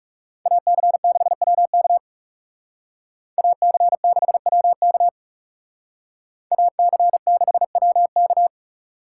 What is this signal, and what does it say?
Morse code, heard as a keyed tone, about 37 wpm, 700 Hz. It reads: AC6WK AC6WK AC6WK
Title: Speed Racing - USA 2x2 Call Signs 30wpm